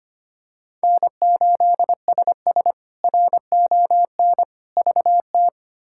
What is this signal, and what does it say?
Morse code, heard as a keyed tone, about 25 wpm, 700 Hz. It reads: N8SS RON VT
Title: K1USN SST QSO Practice - 25wpm